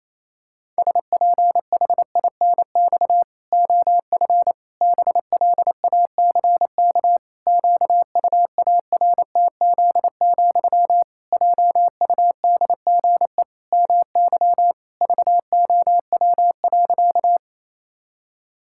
Morse code, heard as a keyed tone, about 28 wpm, 700 Hz. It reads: SPHINX OF BLACK QUARTZ, JUDGE MY VOW.
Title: Pangram Challenge v2 - 28wpm